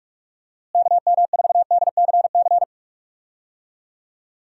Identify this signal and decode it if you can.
Morse code, heard as a keyed tone, about 41 wpm, 700 Hz. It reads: KM4DCC
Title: USA Call Signs - 6 Characters - Repeated 3x - 45wpm